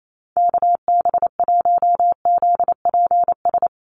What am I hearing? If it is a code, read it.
Morse code, heard as a keyed tone, about 28 wpm, 700 Hz. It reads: KB1ZPS
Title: USA Call Signs Repeated 2x 28wpm